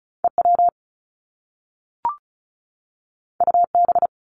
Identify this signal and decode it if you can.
Morse code, heard as a keyed tone, about 36 wpm, 700 Hz. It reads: EW UB
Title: Top Two Letter Combinations in Words 35wpm